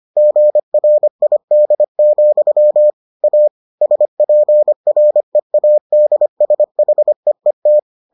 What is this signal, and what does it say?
Morse code, heard as a keyed tone, about 25 wpm, 600 Hz. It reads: GRID, A SPREADSHEET